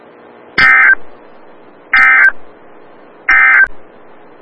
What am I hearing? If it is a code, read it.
Emergency Alert System (SAME) tones, heard as harsh buzzing bursts. It is heard from 0.5 to 3.8 s.
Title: Des Moines, IA NOAA Weather Radio - WXL57